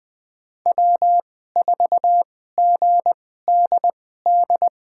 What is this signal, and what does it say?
Morse code, heard as a keyed tone, about 21 wpm, spaced as for 15 wpm, 700 Hz. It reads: W4GDD
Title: USA Call Signs - 5 Characters - 2x Character Spacing - 20wpm